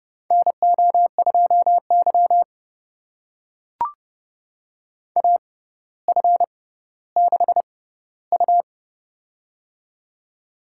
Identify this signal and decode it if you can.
Morse code, heard as a keyed tone, about 30 wpm, spaced as for 15 wpm, 700 Hz. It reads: NO2Y AF6U
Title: USA Call Signs - 4 Characters - 6x Character Spacing - 30wpm